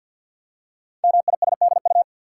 Morse code, heard as a keyed tone, about 48 wpm, 690 Hz. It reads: MISDU
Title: Code Groups - 5 Letters - 50wpm